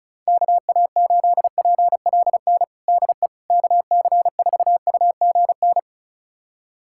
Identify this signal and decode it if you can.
Morse code, heard as a keyed tone, about 34 wpm, 700 Hz. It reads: KA8PLN DE KC4UGN
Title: USA Callsign DE Callsign 35wpm